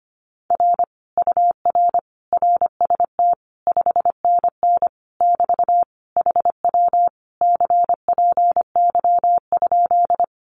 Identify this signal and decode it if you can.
Morse code, heard as a keyed tone, about 25 wpm, 700 Hz. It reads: R UR RST 5NN <BT> HW CPY?
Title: Common QSO Phrases Encore 25wpm